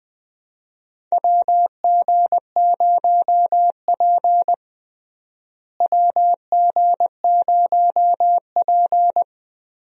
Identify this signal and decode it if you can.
Morse code, heard as a keyed tone, about 20 wpm, 700 Hz. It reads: WG0P WG0P
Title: USA Call Signs - 4 Characters - Repeated 3x - 20wpm